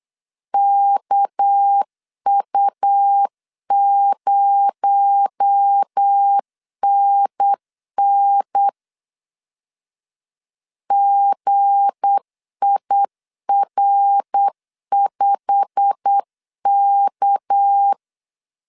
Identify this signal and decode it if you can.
Morse code, heard as a keyed tone, about 8 wpm, 800 Hz. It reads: KU0NN GIR5K
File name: CW / DET / 1 9 1500C S A 8wpm 5637